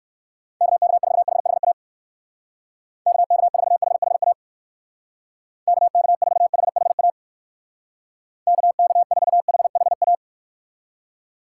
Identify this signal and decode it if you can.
Morse code, heard as a keyed tone, about 59 wpm, 700 Hz. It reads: KK4HHA KK4HHA KK4HHA KK4HHA